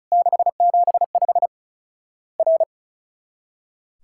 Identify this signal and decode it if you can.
Morse code, heard as a keyed tone, about 36 wpm, 700 Hz. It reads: 675 R
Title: Worked All Europe DX Contest - QTC Exchange - 35wpm